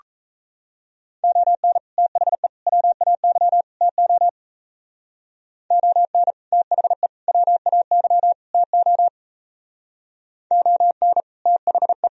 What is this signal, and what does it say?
Morse code, heard as a keyed tone, about 36 wpm, 700 Hz. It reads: ON THE WAY TO ON THE WAY TO ON THE